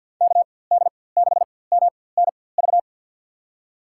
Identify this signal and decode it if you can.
Morse code, heard as a keyed tone, about 52 wpm, spaced as for 23 wpm, 700 Hz. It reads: KD6MNV
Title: USA Call Signs - 6 Characters - 4x Character Spacing - 50wpm